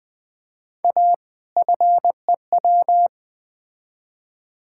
Morse code, heard as a keyed tone, about 20 wpm, 700 Hz. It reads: A FEW
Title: Build A Phrase - 4-Word Phrases - 20wpm